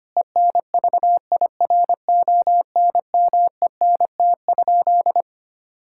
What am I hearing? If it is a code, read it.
Morse code, heard as a keyed tone, about 25 wpm, 700 Hz. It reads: ENVIRONMENT?
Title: Sentences from Top 1000 Words - Review Part 5 - 25wpm